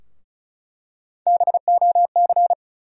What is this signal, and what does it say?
Morse code, heard as a keyed tone, about 36 wpm, 700 Hz. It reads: BOC
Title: USA Call Sign Postfixes - 3 Letters 35wpm